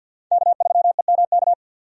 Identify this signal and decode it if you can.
Morse code, heard as a keyed tone, about 46 wpm, 700 Hz. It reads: K3EMK
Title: USA Call Signs - 5 Characters - 50wpm